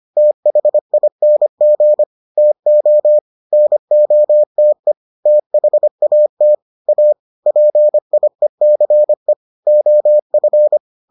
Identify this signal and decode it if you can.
Morse code, heard as a keyed tone, about 25 wpm, 600 Hz. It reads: THING TO NOTE THAT A PIECE OF